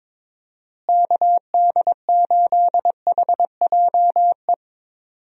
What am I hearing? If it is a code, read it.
Morse code, heard as a keyed tone, about 22 wpm, 700 Hz. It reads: KD8HJE
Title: USA Call Signs - 6 Characters - Repeated 2x - 22wpm